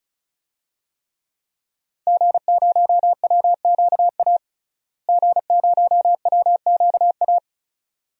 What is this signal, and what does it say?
Morse code, heard as a keyed tone, about 36 wpm, 700 Hz. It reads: G0WQA G0WQA
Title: DXCC-100 Call Signs - 5 Characters - Repeated 2x - 35wpm